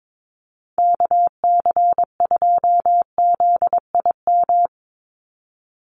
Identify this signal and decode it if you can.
Morse code, heard as a keyed tone, about 22 wpm, 700 Hz. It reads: KC2ZIM
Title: USA Call Signs 22wpm